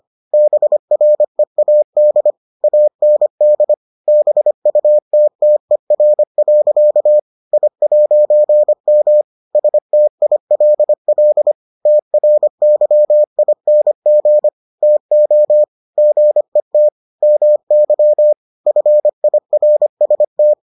Morse code, heard as a keyed tone, about 25 wpm, 600 Hz. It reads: BREAD AND BUTTER. I'M STILL TRYING TO GET MY FIRST